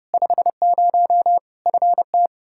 Morse code, heard as a keyed tone, about 30 wpm, 700 Hz. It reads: 50 FT
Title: Practice QSO - Exchange 2 - Info - 30wpm